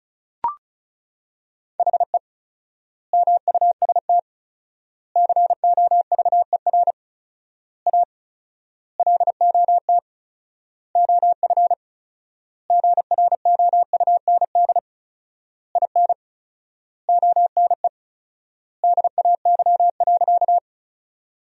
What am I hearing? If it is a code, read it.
Morse code, heard as a keyed tone, about 34 wpm, 700 Hz. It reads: HE MUST COVER A LOT OF GROUND IN ONE DAY.
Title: 4x Word-Spacing - Sentences from Top 500 Words - 35wpm